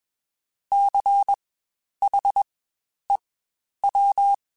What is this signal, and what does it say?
Morse code, heard as a keyed tone, about 21 wpm, spaced as for 10 wpm, 800 Hz. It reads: CHEW
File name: Spaced Alphas / Spaced Alphas mp3s / Cs sp @20